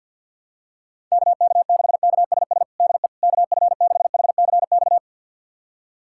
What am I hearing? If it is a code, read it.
Morse code, heard as a keyed tone, about 53 wpm, 700 Hz. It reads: KK6KSS DE KF6HCX
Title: USA Callsign DE Callsign - 6 Characters - 50wpm